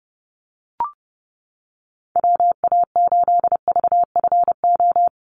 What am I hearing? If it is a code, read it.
Morse code, heard as a keyed tone, about 30 wpm, 700 Hz. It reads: WA8VFO
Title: USA Call Signs Repeated 3x 30wpm